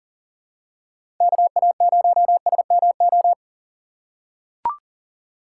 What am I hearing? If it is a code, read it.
Morse code, heard as a keyed tone, about 40 wpm, 700 Hz. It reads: KA0SMO
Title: USA Call Signs - 6 Characters - 3x Character Spacing - 40wpm